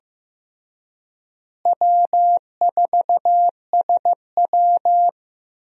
Morse code, heard as a keyed tone, about 15 wpm, 700 Hz. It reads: W4SW
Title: USA Call Signs - 4 Characters - Repeated 3x - 15wpm